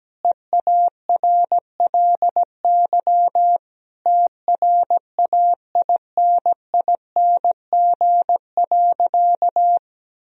Morse code, heard as a keyed tone, about 17 wpm, 700 Hz. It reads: EARLY TRAINING.
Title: Sentences from Top 1000 Words - Review Part 4 - 17wpm